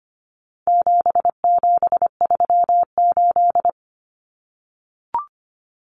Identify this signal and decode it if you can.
Morse code, heard as a keyed tone, about 25 wpm, 700 Hz. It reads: 7738
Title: Numbers - 4-Digit 25wpm